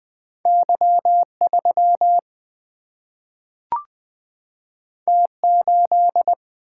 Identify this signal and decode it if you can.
Morse code, heard as a keyed tone, about 20 wpm, 700 Hz. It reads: Y3 T8
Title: Single-Letter Single-Number 20wpm